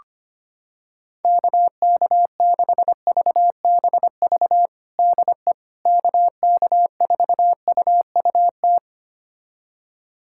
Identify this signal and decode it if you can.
Morse code, heard as a keyed tone, about 25 wpm, 700 Hz. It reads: KK6VBV DE KK4UUT
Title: USA Callsign DE Callsign - 6 Characters - 25wpm